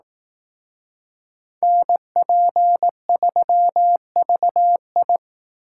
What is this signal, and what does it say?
Morse code, heard as a keyed tone, about 19 wpm, 700 Hz. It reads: NP3VI
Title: Speed Racing - USA 2x2 Call Signs 15wpm